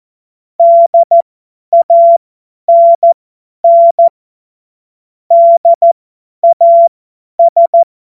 Morse code, heard as a keyed tone, about 14 wpm, spaced as for 10 wpm, 680 Hz. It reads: DANN DAS